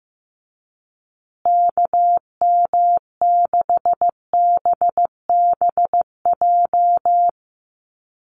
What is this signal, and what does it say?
Morse code, heard as a keyed tone, about 14 wpm, 700 Hz. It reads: KM6BBJ
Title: USA Call Signs Repeated 3x 15wpm